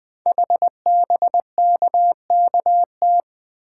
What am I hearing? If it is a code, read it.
Morse code, heard as a keyed tone, about 20 wpm, 700 Hz. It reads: HBKKT
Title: Code Groups - 5 Letters - 20wpm